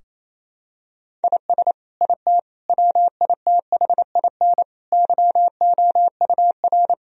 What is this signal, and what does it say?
Morse code, heard as a keyed tone, about 28 wpm, 700 Hz. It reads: IS IT WITHIN YOUR
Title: Sentences from Top 1000 Words - Review Part 4 - 28wpm